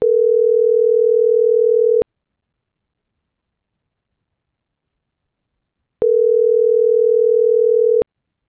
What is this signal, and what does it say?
Ringback tone (North America), 440 + 480 Hz.